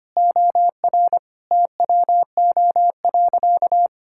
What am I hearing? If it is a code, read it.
Morse code, heard as a keyed tone, about 25 wpm, 700 Hz. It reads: OR TWO.
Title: Sentences from Top 1000 Words - Review Part 5 - 25wpm